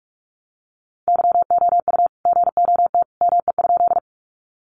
Morse code, heard as a keyed tone, about 44 wpm, 700 Hz. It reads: YOU GOT ME?
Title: Sentences from Top 300 Words 45wpm